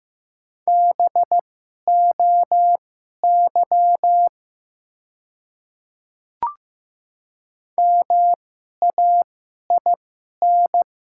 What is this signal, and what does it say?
Morse code, heard as a keyed tone, about 15 wpm, spaced as for 11 wpm, 700 Hz. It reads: BOY MAIN
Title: 2x Character Spacing - Top 300 Words - 15wpm